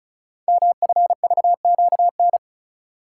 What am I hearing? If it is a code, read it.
Morse code, heard as a keyed tone, about 36 wpm, 700 Hz. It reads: MFVQN
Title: Code Groups - 5 Letters - 35wpm